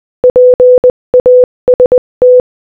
Morse code, heard as a keyed tone, about 20 wpm, 500 Hz. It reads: PAST
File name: Morse-Code-Audio-Speed-20-15